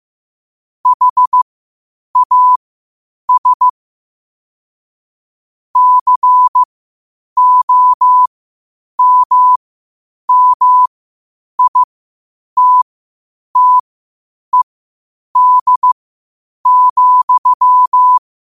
Morse code, heard as a keyed tone, about 15 wpm, spaced as for 8 wpm, 1000 Hz. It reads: HAS COMMITTED,
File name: qotdfiles / 8wpm